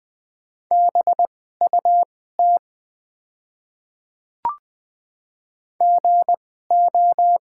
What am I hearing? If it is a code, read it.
Morse code, heard as a keyed tone, about 20 wpm, spaced as for 14 wpm, 700 Hz. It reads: BUT GO
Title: 2x Character Spacing - Top 200 Words - 20wpm